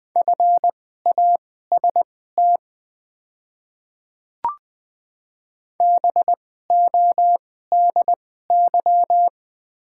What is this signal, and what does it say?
Morse code, heard as a keyed tone, about 20 wpm, spaced as for 14 wpm, 700 Hz. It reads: FAST BODY
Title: 2x Character Spacing - Top 500 Words - 20wpm